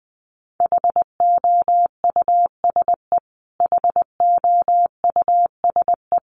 Morse code, heard as a keyed tone, about 20 wpm, 700 Hz. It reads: HOUSE HOUSE
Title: Top 500 Words Repeated 20wpm